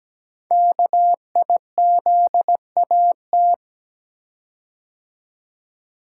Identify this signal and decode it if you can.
Morse code, heard as a keyed tone, about 17 wpm, 700 Hz. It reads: KIZAT
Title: Code Groups - 5 Letters - 17wpm